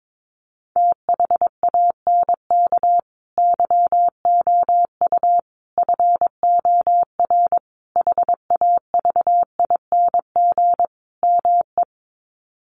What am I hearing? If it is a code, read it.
Morse code, heard as a keyed tone, about 22 wpm, 700 Hz. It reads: THANK YOU FOR HAVING ME
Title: Sets of 5 Words Encore 22wpm